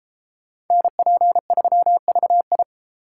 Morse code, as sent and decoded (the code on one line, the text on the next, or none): -. .--. ...-- ...- ..
NP3VI